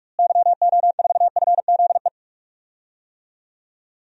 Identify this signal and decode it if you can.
Morse code, heard as a keyed tone, about 44 wpm, 700 Hz. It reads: YO4FZE
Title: DXCC-100 Call Signs - 6 Characters - Repeated 2x - 45wpm